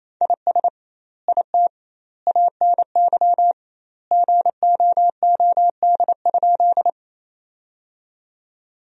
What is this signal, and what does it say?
Morse code, heard as a keyed tone, about 27 wpm, 700 Hz. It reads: IS IT ANY GOOD?
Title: 2x Word-Spacing - Sentences from Top 200 Words - 28wpm